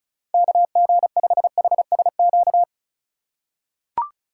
Morse code, heard as a keyed tone, about 36 wpm, 700 Hz. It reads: KG5HSQ
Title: USA Call Signs - 6 Characters - 3x Character Spacing - 35wpm